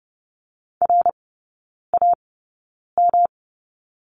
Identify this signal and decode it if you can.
Morse code, heard as a keyed tone, about 30 wpm, spaced as for 9 wpm, 700 Hz. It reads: RAM
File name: Lesson 18e - Letter m - Three Characters 30wpm